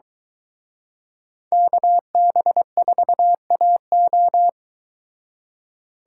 Morse code, heard as a keyed tone, about 23 wpm, 700 Hz. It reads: KB4AO